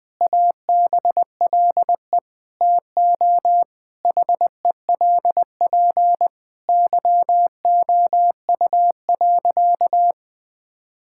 Morse code, heard as a keyed tone, about 20 wpm, 700 Hz. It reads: ABLE TO HELP YOU.
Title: Sentences with 4-Word Phrases - Spoken Beginning - 20wpm